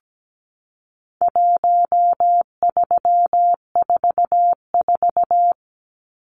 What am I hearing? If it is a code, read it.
Morse code, heard as a keyed tone, about 17 wpm, 700 Hz. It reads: 1344
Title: Numbers - 4-Digit 17wpm